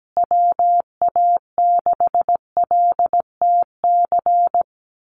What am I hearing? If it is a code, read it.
Morse code, heard as a keyed tone, about 17 wpm, 700 Hz. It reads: WA6LTC
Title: USA Call Signs Repeated 3x 17wpm